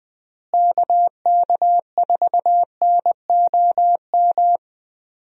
Morse code, heard as a keyed tone, about 20 wpm, 700 Hz. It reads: KK4NOM
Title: USA Call Signs - 6 Characters - Repeated 3x - 20wpm